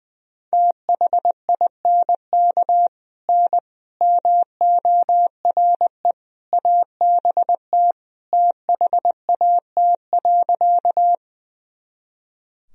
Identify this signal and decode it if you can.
Morse code, heard as a keyed tone, about 21 wpm, 700 Hz. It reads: THINK N MORE ABT THAT.
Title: Sentences from Top 200 Words and Abbreviations 20wpm